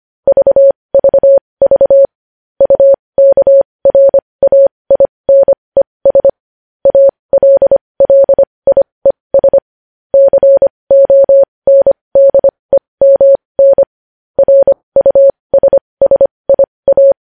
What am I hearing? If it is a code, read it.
Morse code, heard as a keyed tone, about 25 wpm, spaced as for 20 wpm, 550 Hz. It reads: VVV UKRAINES ALLIES CONDEMN RUSSIA
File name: podcast / News headlines-1x-25x20